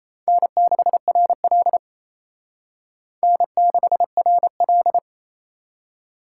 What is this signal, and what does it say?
Morse code, heard as a keyed tone, about 35 wpm, 700 Hz. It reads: N6RL N6RL
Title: Speed Racing - USA Call Signs 28wpm